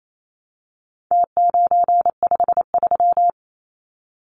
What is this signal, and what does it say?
Morse code, heard as a keyed tone, about 28 wpm, 700 Hz. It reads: T953
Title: Numbers - Contest Style - 4-Digit 28wpm